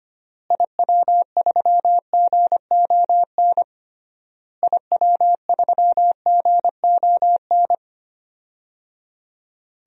Morse code, heard as a keyed tone, about 25 wpm, 700 Hz. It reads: IW3GON IW3GON
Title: DXCC-100 Call Signs - 6 Characters - Repeated 3x - 25wpm